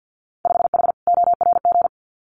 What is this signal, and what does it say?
Morse code, heard as a keyed tone, about 50 wpm, 700 Hz. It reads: 5H ORG
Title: Winter Field Day Exchanges Encore 50wpm